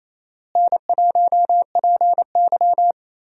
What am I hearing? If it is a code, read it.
Morse code, heard as a keyed tone, about 28 wpm, 700 Hz. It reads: N1PY
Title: USA Call Signs - 4 Characters - Repeated 2x - 28wpm